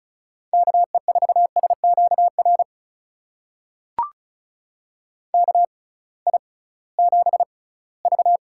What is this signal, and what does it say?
Morse code, heard as a keyed tone, about 36 wpm, spaced as for 20 wpm, 700 Hz. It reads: KE4SQR KI7V
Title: USA Call Signs - 6 Characters - 6x Character Spacing - 35wpm